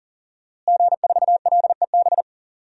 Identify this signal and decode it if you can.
Morse code, heard as a keyed tone, about 40 wpm, 700 Hz. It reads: G4LEB
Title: DXCC-100 Call Signs - 5 Characters - Repeated 3x - 40wpm